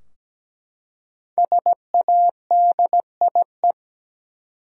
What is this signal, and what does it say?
Morse code, heard as a keyed tone, about 17 wpm, 700 Hz. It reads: SADIE
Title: USA Common Names - Max 5 Letters - 17wpm